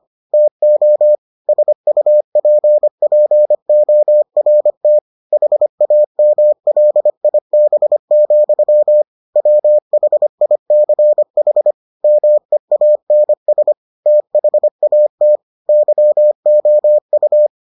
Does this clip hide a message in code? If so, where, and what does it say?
Yes: Morse code from 0.3 to 17.5 s, heard as a keyed tone, about 25 wpm, 600 Hz. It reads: TO SUPPORT HAMLIB, WHICH MEANS THAT YOU